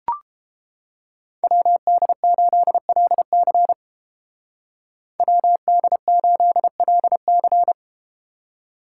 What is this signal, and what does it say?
Morse code, heard as a keyed tone, about 30 wpm, 700 Hz. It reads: WD8LC WD8LC